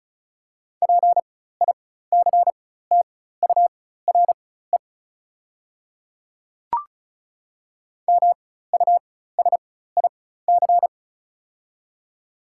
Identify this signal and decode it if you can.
Morse code, heard as a keyed tone, about 34 wpm, spaced as for 16 wpm, 690 Hz. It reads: PICTURE MUSIC
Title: 4x Character Spacing - Top 300 Words - 35wpm